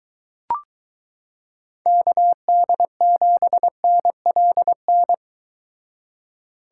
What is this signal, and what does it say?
Morse code, heard as a keyed tone, about 22 wpm, 700 Hz. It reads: KD7NLN